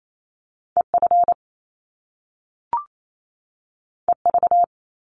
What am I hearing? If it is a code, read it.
Morse code, heard as a keyed tone, about 28 wpm, 700 Hz. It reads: EF EV
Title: Top Two Letter Combinations in Words 28wpm